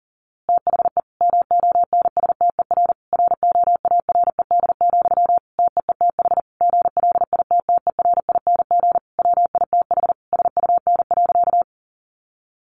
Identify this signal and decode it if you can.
Morse code, heard as a keyed tone, about 40 wpm, 700 Hz. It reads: THE MONSTER ROARED, TEETH GLITTERING WITH SUN.
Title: A Sound Of Thunder 40wpm